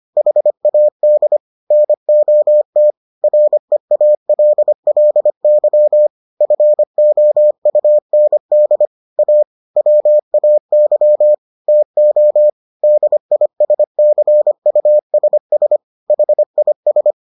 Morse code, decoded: HAD NOT REALLY FOUND A WAY TO DISCUSS HIS

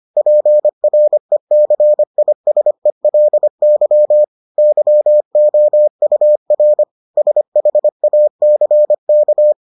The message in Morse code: PRECISELY YOUR SHACK

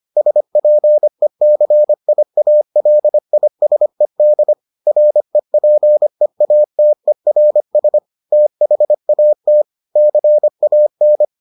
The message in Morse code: SPECIALISED REPEATERS THAT CAN

